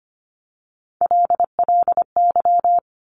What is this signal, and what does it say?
Morse code, heard as a keyed tone, about 25 wpm, 700 Hz. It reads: LLY
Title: Top Three Letter Combinations in Words 25wpm